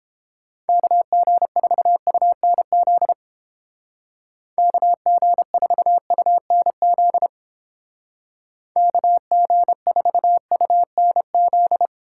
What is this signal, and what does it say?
Morse code, heard as a keyed tone, about 29 wpm, 700 Hz. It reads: KG4UNZ KG4UNZ KG4UNZ